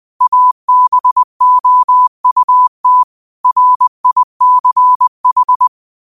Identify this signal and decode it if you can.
Morse code, heard as a keyed tone, about 20 wpm, 1000 Hz. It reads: ABOUT RICH
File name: qotdfiles / 20wpm